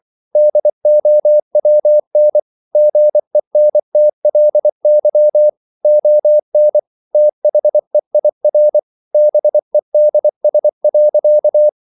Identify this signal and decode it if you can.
Morse code, heard as a keyed tone, about 24 wpm, 600 Hz. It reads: DOWN GENTLY ON THEIR BEDS.